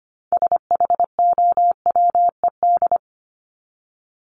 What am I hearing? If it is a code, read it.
Morse code, heard as a keyed tone, about 25 wpm, 700 Hz. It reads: SHOWED
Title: ED Suffix - Top 100 Words with Base Words 25wpm